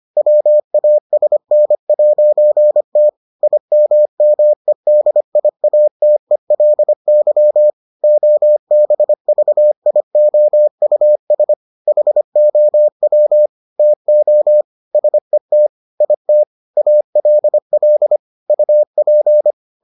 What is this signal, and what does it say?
Morse code, heard as a keyed tone, about 25 wpm, 600 Hz. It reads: WASN'T IMMEDIATELY OBVIOUS HOW TO SET IT ALL UP